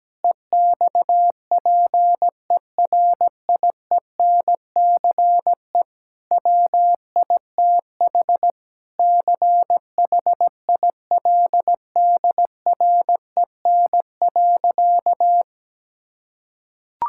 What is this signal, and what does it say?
Morse code, heard as a keyed tone, about 17 wpm, 700 Hz. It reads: EXPERIENCE WITH CHILDREN.
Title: Sentences from Top 1000 Words - Review Part 5 - 17wpm